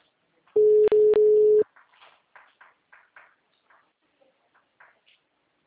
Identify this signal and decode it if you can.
Ringback tone (Europe, 425 Hz).